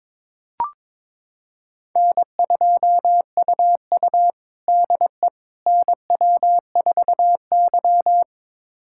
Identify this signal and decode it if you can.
Morse code, heard as a keyed tone, about 22 wpm, 700 Hz. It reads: N2UU DE NW4Y
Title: USA Callsign DE Callsign - 4 Characters - 22wpm